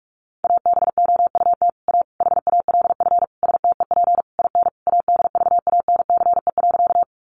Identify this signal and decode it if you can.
Morse code, heard as a keyed tone, about 44 wpm, 700 Hz. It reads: ABOUT A HALF STEP IN ADVANCE.
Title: Sentences from Top 1400 Words 45wpm